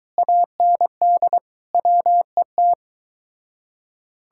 Morse code, heard as a keyed tone, about 23 wpm, 700 Hz. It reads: AND WET